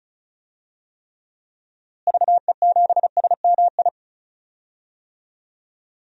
Morse code, heard as a keyed tone, about 36 wpm, 700 Hz. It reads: VE7SMI
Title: DXCC-100 Call Signs - 6 Characters - Repeated 2x - 35wpm